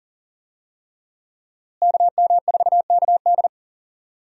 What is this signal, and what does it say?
Morse code, heard as a keyed tone, about 37 wpm, 700 Hz. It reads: KM4KD
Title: USA Call Signs - 5 Characters - Repeated 3x - 40wpm